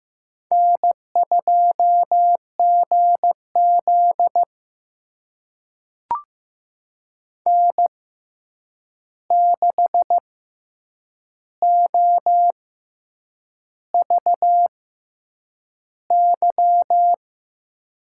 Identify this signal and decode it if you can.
Morse code, heard as a keyed tone, about 15 wpm, spaced as for 7 wpm, 700 Hz. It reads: N2GZ N6OVY
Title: USA Call Signs - 6x Character Spacing - 15wpm